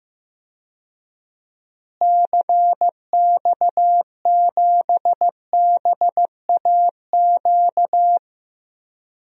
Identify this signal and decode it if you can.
Morse code, heard as a keyed tone, about 16 wpm, 700 Hz. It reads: CX7BAQ
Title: DXCC-100 Call Signs - 6 Characters - Repeated 2x - 15wpm